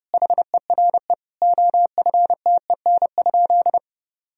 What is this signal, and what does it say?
Morse code, heard as a keyed tone, about 30 wpm, 700 Hz. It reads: HERE OFTEN?
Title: Sentences from Top 1000 Words - Review Part 1 - 30wpm